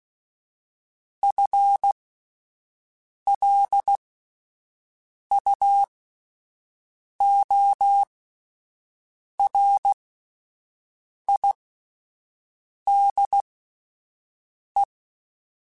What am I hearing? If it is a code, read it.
Morse code, heard as a keyed tone, about 16 wpm, spaced as for 5 wpm, 800 Hz. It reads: FLUORIDE
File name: Spaced / FL Runs sp mp3s / FL Runs sp @10